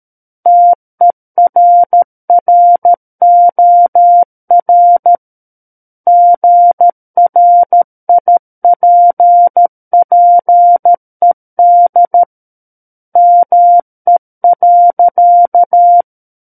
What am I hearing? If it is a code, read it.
Morse code, heard as a keyed tone, about 13 wpm, 700 Hz. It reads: TERROR GRIPPED ME.
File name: Book One / 04-Chapter 4 - 13 WPM